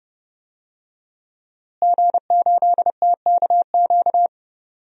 Morse code, heard as a keyed tone, about 30 wpm, 700 Hz. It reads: G8TKQ
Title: DXCC-100 Call Signs - 5 Characters - Repeated 2x - 30wpm